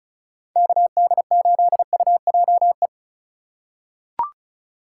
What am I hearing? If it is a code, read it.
Morse code, heard as a keyed tone, about 36 wpm, 700 Hz. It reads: KD8UJE
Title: USA Call Signs - 6 Characters - 2x Character Spacing - 35wpm